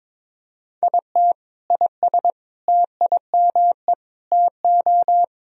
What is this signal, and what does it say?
Morse code, heard as a keyed tone, about 22 wpm, 700 Hz. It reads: IT IS TIME TO
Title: Sentences from Top 1000 Words - Review Part 1 - 22wpm